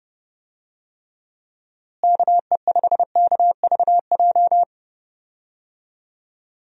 Morse code, heard as a keyed tone, about 32 wpm, 700 Hz. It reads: KE5KVJ
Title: USA Call Signs - 6 Characters - Repeated 3x - 30wpm